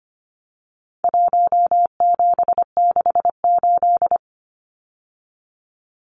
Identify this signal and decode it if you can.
Morse code, heard as a keyed tone, about 25 wpm, 700 Hz. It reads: 1768
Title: Numbers - Contest Style - 4-Digit 25wpm